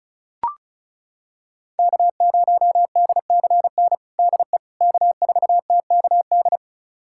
Morse code, heard as a keyed tone, about 34 wpm, 700 Hz. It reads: K0DCN DE K4TKD
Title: USA Callsign DE Callsign - 5 Characters - 35wpm